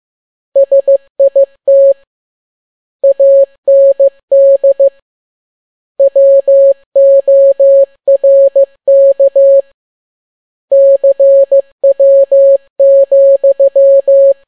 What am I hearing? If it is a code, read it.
Morse code, heard as a keyed tone, about 15 wpm, 550 Hz. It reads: SIT AND WORK CW,
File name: n6tt p1 15C13